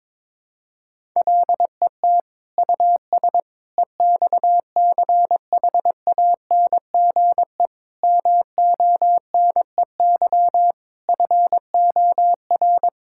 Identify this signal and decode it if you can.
Morse code, heard as a keyed tone, about 22 wpm, 700 Hz. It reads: LET US EXCHANGE MONEY FOR
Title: Sentences from Top 1000 Words - Review Part 6 - 22wpm